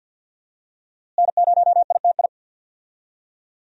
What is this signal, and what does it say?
Morse code, heard as a keyed tone, about 51 wpm, 700 Hz. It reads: N0ITI
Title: USA Call Signs - 5 Characters - Repeated 2x - 50wpm